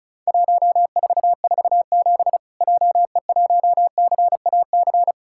Morse code, heard as a keyed tone, about 34 wpm, 700 Hz. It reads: 1447 JE1CAC
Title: Worked All Europe DX Contest - QTC Exchange - 35wpm